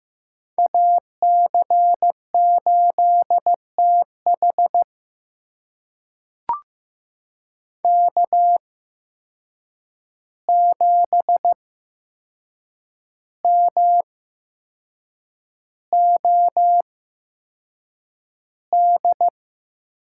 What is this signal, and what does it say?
Morse code, heard as a keyed tone, about 15 wpm, spaced as for 6 wpm, 700 Hz. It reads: AC8TH K7MOD
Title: USA Call Signs - 5 Characters - 8x Character Spacing - 15wpm